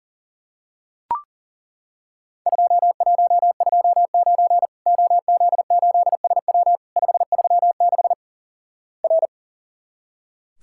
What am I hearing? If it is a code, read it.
Morse code, heard as a keyed tone, about 38 wpm, 700 Hz. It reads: 2129 OZ8SW 536 R
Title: Worked All Europe DX Contest - QTC Exchange - 40wpm